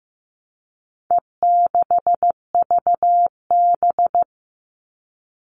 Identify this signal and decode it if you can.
Morse code, heard as a keyed tone, about 15 wpm, 700 Hz. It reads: E6VB
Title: All Cut Numbers - 4-Digit 15wpm